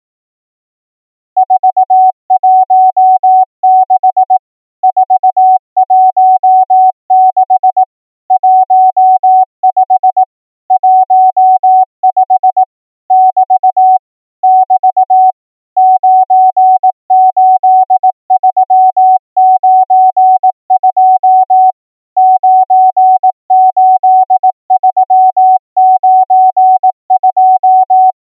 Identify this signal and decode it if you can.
Morse code, heard as a keyed tone, about 18 wpm, 750 Hz. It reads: 416 416 15 15 <BT> <BT> 98392 98392
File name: numbers / V2-morse